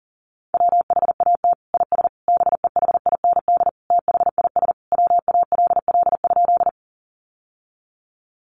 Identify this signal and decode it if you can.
Morse code, heard as a keyed tone, about 40 wpm, 700 Hz. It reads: WHAT IS BEHIND THIS WALL?
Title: Sentences from Top 1100 Words 40wpm